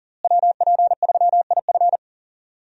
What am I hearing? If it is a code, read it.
Morse code, heard as a keyed tone, about 40 wpm, 700 Hz. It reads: WP3IF
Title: USA Call Signs - 5 Characters - Repeated 2x - 40wpm